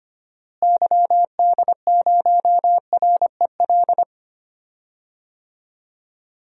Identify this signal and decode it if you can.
Morse code, heard as a keyed tone, about 25 wpm, 700 Hz. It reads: YD0REL
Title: DXCC-100 Call Signs - 6 Characters - Repeated 2x - 25wpm